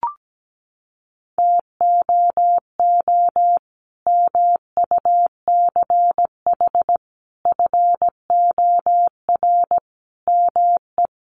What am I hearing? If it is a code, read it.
Morse code, heard as a keyed tone, about 17 wpm, 700 Hz. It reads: TOO MUCH FOR ME
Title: Sentences from Top 200 Words 17wpm